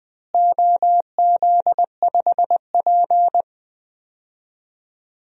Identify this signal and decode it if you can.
Morse code, heard as a keyed tone, about 20 wpm, 700 Hz. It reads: OZ5P
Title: DXCC-100 Call Signs - 4 Characters - Repeated 3x - 20wpm